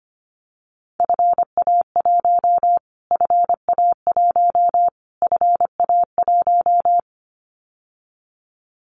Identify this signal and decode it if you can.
Morse code, heard as a keyed tone, about 25 wpm, 700 Hz. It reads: FA1 FA1 FA1
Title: Callsign Prefixes for Top 25 DXCC - Repeated 3x - 25wpm